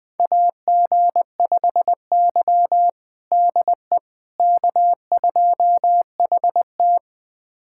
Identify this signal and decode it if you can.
Morse code, heard as a keyed tone, about 21 wpm, 700 Hz. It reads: AG5Y DE K2HT
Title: USA Callsign DE Callsign - 4 Characters - 20wpm